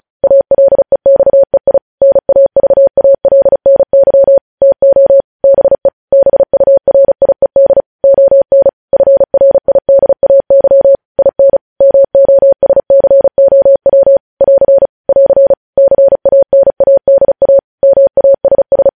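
Morse code, heard as a keyed tone, about 36 wpm, 550 Hz. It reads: ALEXEI NAVALNY TO BE BURIED ON FRIDAY IN MOSCOW <AR> <AR> CANADA MASS
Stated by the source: Morse news headlines practice at 35 words per minute.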